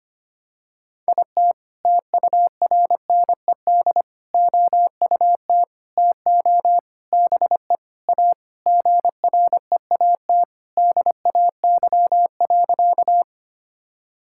Morse code, heard as a keyed tone, about 25 wpm, 700 Hz. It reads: IT TURNED OUT TO BE A GREAT DAY.
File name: practice / 25wpm / Sentences with 4-Word Phrases - 25wpm